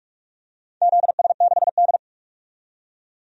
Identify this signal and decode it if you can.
Morse code, heard as a keyed tone, about 41 wpm, 700 Hz. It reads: ZS6D
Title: DXCC-100 Call Signs - 4 Characters - 45wpm